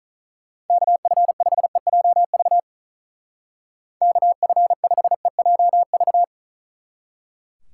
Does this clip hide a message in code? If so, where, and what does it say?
Yes: Morse code from 0.7 to 6.2 s, heard as a keyed tone, about 44 wpm, 700 Hz. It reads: KF5EJV KF5EJV